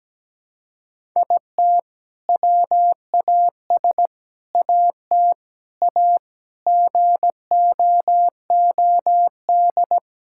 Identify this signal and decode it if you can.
Morse code, heard as a keyed tone, about 17 wpm, 700 Hz. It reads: IT WAS AT A GOOD
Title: Sentences from Top 1000 Words - Review Part 5 - 17wpm